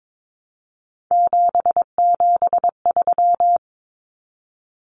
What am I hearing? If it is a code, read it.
Morse code, heard as a keyed tone, about 22 wpm, 700 Hz. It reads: 773
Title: Common Cut Numbers - 3-Digit 22wpm